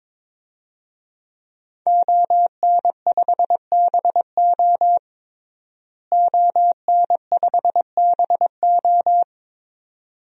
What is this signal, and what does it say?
Morse code, heard as a keyed tone, about 22 wpm, 700 Hz. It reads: ON5BO ON5BO
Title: DXCC-100 Call Signs - 5 Characters - Repeated 2x - 22wpm